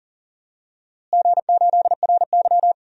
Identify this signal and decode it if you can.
Morse code, heard as a keyed tone, about 40 wpm, 700 Hz. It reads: G8RY
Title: DXCC-100 Call Signs - 4 Characters - Repeated 2x - 40wpm